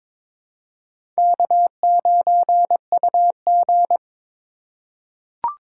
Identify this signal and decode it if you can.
Morse code, heard as a keyed tone, about 22 wpm, 700 Hz. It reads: K9UG
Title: USA Call Signs - 4 Characters - 4x Character Spacing - 22wpm